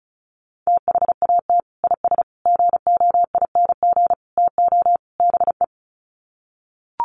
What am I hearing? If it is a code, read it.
Morse code, heard as a keyed tone, about 34 wpm, 700 Hz. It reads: THAT IS GOING TO BE
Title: Sets of 5 Words Encore2 35wpm